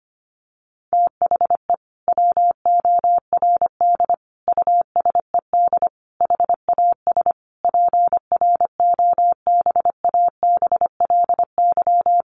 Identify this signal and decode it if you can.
Morse code, heard as a keyed tone, about 25 wpm, 700 Hz. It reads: THE WORD USED HAS PROBABLY
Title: Sentences from Top 1100 Words 25wpm